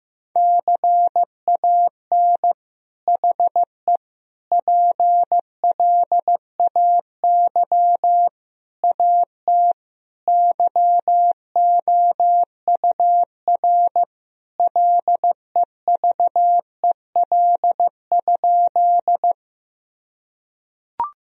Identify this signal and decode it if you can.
Morse code, heard as a keyed tone, about 15 wpm, 700 Hz. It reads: CAN HE PLAY AT YOUR LEVEL?
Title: Sentences from Top 1000 Words - Review Part 4 - 15wpm